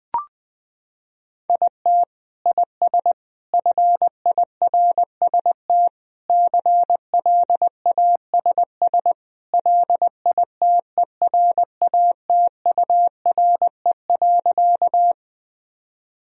Morse code, heard as a keyed tone, about 20 wpm, 700 Hz. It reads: IT IS FIRST CLASS LITERATURE.
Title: Sentences from Top 1000 Words - Review Part 3 - 20wpm